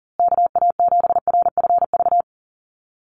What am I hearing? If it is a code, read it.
Morse code, heard as a keyed tone, about 40 wpm, 700 Hz. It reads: KA7RFV
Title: USA Call Signs Repeated 2x 40wpm